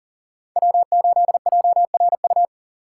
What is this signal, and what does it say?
Morse code, heard as a keyed tone, about 52 wpm, 700 Hz. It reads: W8JRU